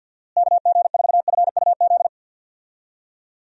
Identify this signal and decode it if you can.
Morse code, heard as a keyed tone, about 52 wpm, 700 Hz. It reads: KG4FUZ
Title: USA Call Signs - 6 Characters - Repeated 2x - 50wpm